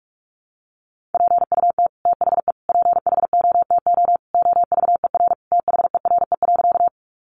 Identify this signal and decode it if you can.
Morse code, heard as a keyed tone, about 44 wpm, 700 Hz. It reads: PUT THE PHOTO OVER THERE.
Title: Sentences from Top 1200 Words 45wpm